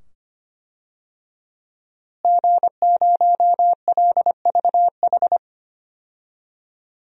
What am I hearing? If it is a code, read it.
Morse code, heard as a keyed tone, about 25 wpm, 700 Hz. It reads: G0LVH
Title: DXCC-100 Call Signs - 5 Characters - Repeated 2x - 25wpm